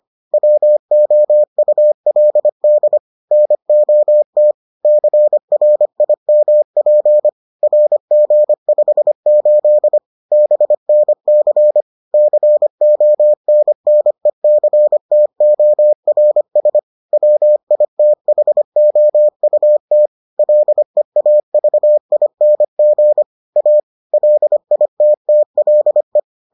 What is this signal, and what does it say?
Morse code, heard as a keyed tone, about 25 wpm, 600 Hz. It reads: WOULD NOT CRIMP RG58 BNC CONNECTORS WITHOUT LEAVING A LITTLE